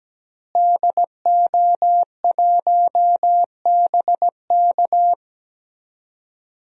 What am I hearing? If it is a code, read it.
Morse code, heard as a keyed tone, about 17 wpm, 700 Hz. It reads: DO1BK
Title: DXCC-100 Call Signs - 5 Characters - Repeated 3x - 17wpm